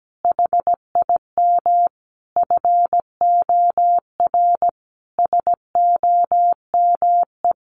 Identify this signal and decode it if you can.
Morse code, heard as a keyed tone, about 17 wpm, 700 Hz. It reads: HIM FOR SOME
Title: Sentences from Top 600 Words 17wpm